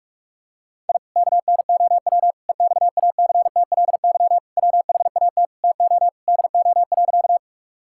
Morse code, heard as a keyed tone, about 42 wpm, 700 Hz. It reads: I KNOW EXACTLY WHAT TO DO.
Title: Sentences from Top 1000 Words - Review Part 6 - 45wpm